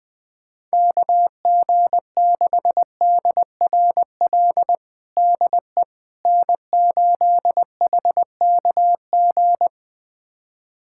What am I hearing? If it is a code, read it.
Morse code, heard as a keyed tone, about 20 wpm, 700 Hz. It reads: KG6DRL DE N8HKG
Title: USA Callsign DE Callsign 20wpm